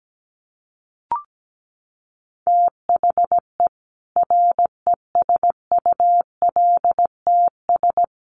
Morse code, heard as a keyed tone, about 17 wpm, 700 Hz. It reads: THE RESULTS
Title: Sets of 4 Words 17wpm